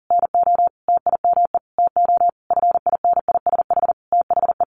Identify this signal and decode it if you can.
Morse code, heard as a keyed tone, about 40 wpm, 700 Hz. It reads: NO TIME TO FINISH THE
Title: Sentences from Top 1200 Words 40wpm